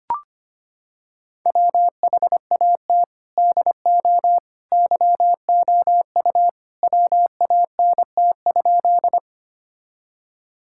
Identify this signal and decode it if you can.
Morse code, heard as a keyed tone, about 25 wpm, 700 Hz. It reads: WHAT DO YOU WANT?
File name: practice / 25wpm / Sentences with 2-Word Phrases - 25wpm